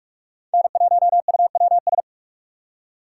Morse code, heard as a keyed tone, about 59 wpm, 700 Hz. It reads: N1UWS